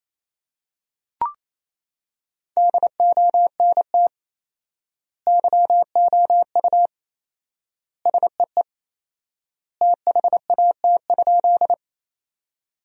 Morse code, heard as a keyed tone, about 28 wpm, 700 Hz. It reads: DONT YOU SEE THAT?
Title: 4x Word-Spacing - Sentences from Top 300 Words - 28wpm